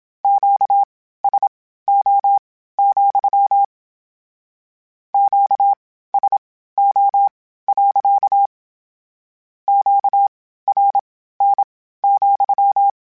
Morse code, heard as a keyed tone, about 27 wpm, spaced as for 15 wpm, 800 Hz. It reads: QSO, QSO. QRN,
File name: Q Calls / Q Calls-mp3s / Q Calls sppc @25